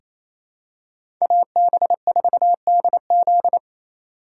Morse code, heard as a keyed tone, about 28 wpm, 700 Hz. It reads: AB4DZ
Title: USA Call Signs - 5 Characters - 4x Character Spacing - 28wpm